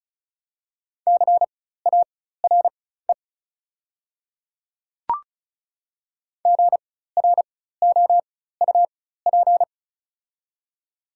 Morse code, heard as a keyed tone, about 36 wpm, spaced as for 16 wpm, 700 Hz. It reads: CARE GROUP
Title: 4x Character Spacing - Top 300 Words - 35wpm